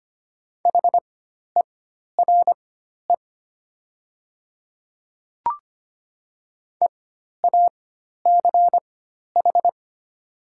Morse code, heard as a keyed tone, about 25 wpm, spaced as for 12 wpm, 700 Hz. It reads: HERE EACH
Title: 4x Character Spacing - Top 300 Words - 25wpm